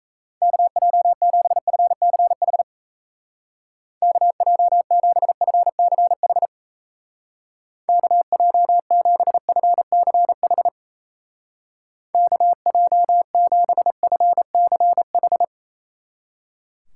Morse code, heard as a keyed tone, about 36 wpm, 700 Hz. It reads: KJ7FCH KJ7FCH KJ7FCH KJ7FCH